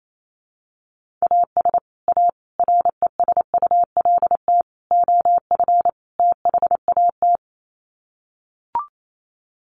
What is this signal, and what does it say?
Morse code, heard as a keyed tone, about 28 wpm, 700 Hz. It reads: AS A RESULT OF THAT
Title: Sets of 5 Words Encore 28wpm